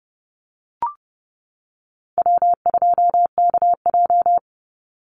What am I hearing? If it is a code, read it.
Morse code, heard as a keyed tone, about 30 wpm, 700 Hz. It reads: W2KJ
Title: Call Signs - 30wpm